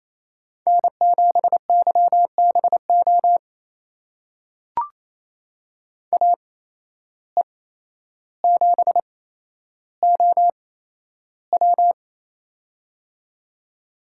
Morse code, heard as a keyed tone, about 28 wpm, spaced as for 12 wpm, 700 Hz. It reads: N7YBO AE7OW
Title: USA Call Signs - 5 Characters - 8x Character Spacing - 28wpm